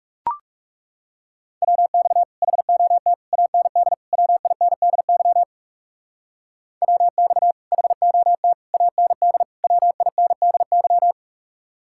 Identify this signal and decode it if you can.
Morse code, heard as a keyed tone, about 37 wpm, 700 Hz. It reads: WX HOT AND WINDY WX HOT AND WINDY
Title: Speed-Racing Common QSO Phrases Encore - 30wpm